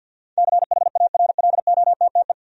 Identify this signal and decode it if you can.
Morse code, heard as a keyed tone, about 52 wpm, 700 Hz. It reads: CHARLOTTE
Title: USA Common Names - 50wpm